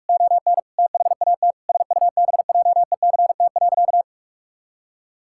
Morse code, heard as a keyed tone, about 46 wpm, 700 Hz. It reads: ON THAT SUBJECT.
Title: Sentences from Top 1600 Words 45wpm